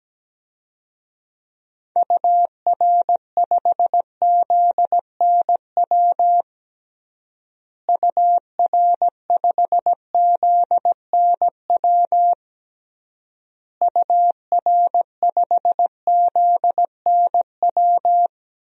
Morse code, heard as a keyed tone, about 17 wpm, 700 Hz. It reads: UR5ZNW UR5ZNW UR5ZNW
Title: DXCC-100 Call Signs - 6 Characters - Repeated 3x - 17wpm